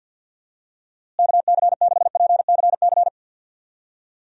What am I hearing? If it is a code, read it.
Morse code, heard as a keyed tone, about 50 wpm, 700 Hz. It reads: KC6PCC
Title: USA Call Signs - 6 Characters - Repeated 3x - 50wpm